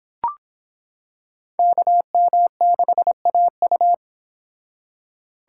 Morse code, heard as a keyed tone, about 26 wpm, 700 Hz. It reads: KM6AU